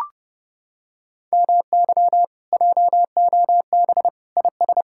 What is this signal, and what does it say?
Morse code, heard as a keyed tone, about 30 wpm, 700 Hz. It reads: MY JOB IS